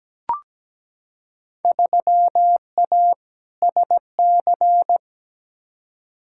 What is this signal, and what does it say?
Morse code, heard as a keyed tone, about 17 wpm, 700 Hz. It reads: 3A SC
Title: Field Day Exchanges - 17wpm